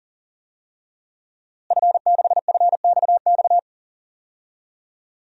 Morse code, heard as a keyed tone, about 40 wpm, 700 Hz. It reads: F6FXX
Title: DXCC-100 Call Signs - 5 Characters - Repeated 3x - 40wpm